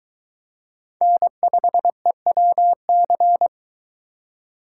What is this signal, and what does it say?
Morse code, heard as a keyed tone, about 22 wpm, 700 Hz. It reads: N5EWC